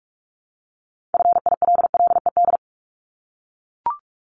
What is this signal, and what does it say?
Morse code, heard as a keyed tone, about 44 wpm, 700 Hz. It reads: FILLED
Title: ED Suffix - Top 100 Words with Base Words 45wpm